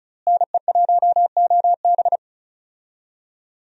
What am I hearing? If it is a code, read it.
Morse code, heard as a keyed tone, about 34 wpm, 700 Hz. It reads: NE1OB
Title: USA Call Signs - 5 Characters - Repeated 2x - 35wpm